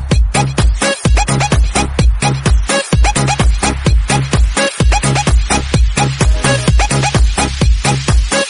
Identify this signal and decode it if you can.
There is no code.